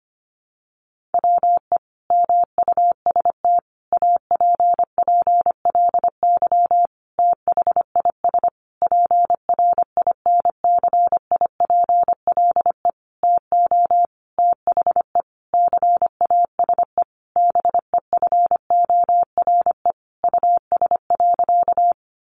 Morse code, heard as a keyed tone, about 25 wpm, 700 Hz. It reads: WE MUST APPLY THIS PRINCIPLE TO THE CASE BEFORE US.
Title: Sentences from Top 1400 Words 25wpm